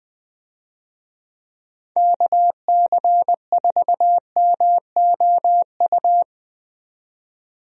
Morse code, heard as a keyed tone, about 20 wpm, 700 Hz. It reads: KC4MOU
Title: USA Call Signs - 6 Characters - 20wpm